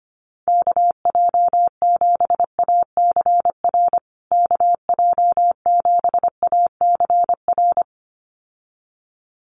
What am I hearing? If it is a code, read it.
Morse code, heard as a keyed tone, about 25 wpm, 700 Hz. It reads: KJ7ACR KJ7ACR
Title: USA Call Signs Repeated 2x 25wpm